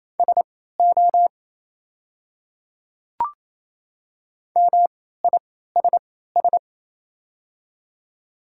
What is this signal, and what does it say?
Morse code, heard as a keyed tone, about 28 wpm, spaced as for 16 wpm, 700 Hz. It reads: SO MISS
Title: 3x Character Spacing - Top 500 Words - 28wpm